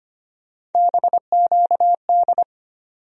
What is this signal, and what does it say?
Morse code, heard as a keyed tone, about 25 wpm, 700 Hz. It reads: BQD
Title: USA Call Sign Postfixes - 3 Letters 25wpm